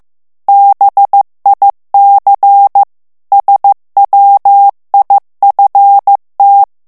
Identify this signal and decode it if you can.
Morse code, heard as a keyed tone, about 15 wpm, 800 Hz. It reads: BIC SWIFT
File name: ITALIEN texte clair 1 vitesse 14 mots minute